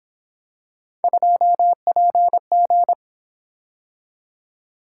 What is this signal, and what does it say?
Morse code, heard as a keyed tone, about 26 wpm, 700 Hz. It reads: 2PG
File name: Speed-Racing - Three Letter-Number Combinations - 22wpm